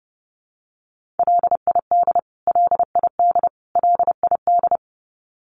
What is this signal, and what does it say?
Morse code, heard as a keyed tone, about 30 wpm, 700 Hz. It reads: LID LID LID
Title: Abbreviations Repeated 3x - 30wpm